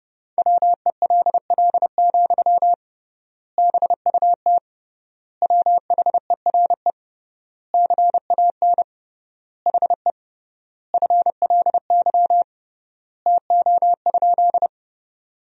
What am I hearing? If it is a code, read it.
Morse code, heard as a keyed tone, about 30 wpm, 700 Hz. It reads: WELL, BUT WHERE CAN HE FLY TO?
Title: 3x Word-Spacing - Sentences from Top 500 Words - 30wpm